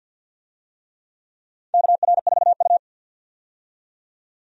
Morse code, heard as a keyed tone, about 52 wpm, 700 Hz. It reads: KR4U
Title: USA Call Signs - 4 Characters - 50wpm